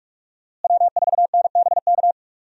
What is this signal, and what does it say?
Morse code, heard as a keyed tone, about 46 wpm, 700 Hz. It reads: W4NBK
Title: USA Call Signs - 5 Characters - Repeated 2x - 45wpm